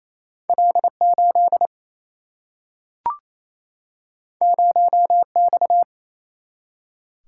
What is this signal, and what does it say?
Morse code, heard as a keyed tone, about 28 wpm, 700 Hz. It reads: L 8 0 X
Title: Single-Letter Single-Number 28wpm